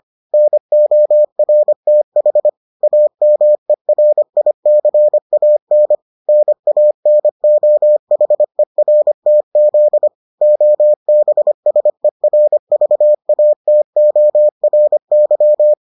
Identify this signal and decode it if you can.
Morse code, heard as a keyed tone, about 25 wpm, 600 Hz. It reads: NORTH AMERICAN NANOHERTZ OBSERVATORY